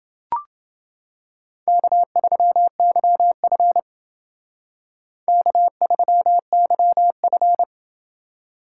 Morse code, heard as a keyed tone, about 29 wpm, 700 Hz. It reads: K3YF K3YF